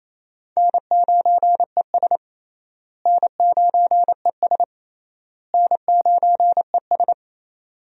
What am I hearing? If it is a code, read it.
Morse code, heard as a keyed tone, about 28 wpm, 700 Hz. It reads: N9ES N9ES N9ES
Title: USA Call Signs - 4 Characters - Repeated 3x - 28wpm